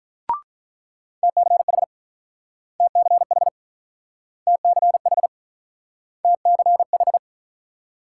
Morse code, heard as a keyed tone, about 47 wpm, 700 Hz. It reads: TCH TCH TCH TCH